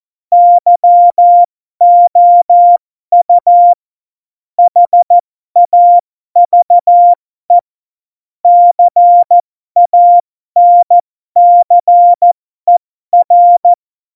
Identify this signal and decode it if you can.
Morse code, heard as a keyed tone, about 14 wpm, spaced as for 12 wpm, 700 Hz. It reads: YOU HAVE CANCER
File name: Morsekurs / Witze